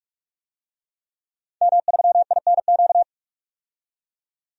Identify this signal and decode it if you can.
Morse code, heard as a keyed tone, about 43 wpm, 700 Hz. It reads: M3INQ
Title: DXCC-100 Call Signs - 5 Characters - 45wpm